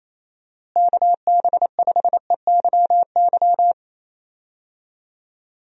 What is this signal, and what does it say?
Morse code, heard as a keyed tone, about 28 wpm, 700 Hz. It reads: KB5EYY
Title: USA Call Signs - 6 Characters - Repeated 2x - 28wpm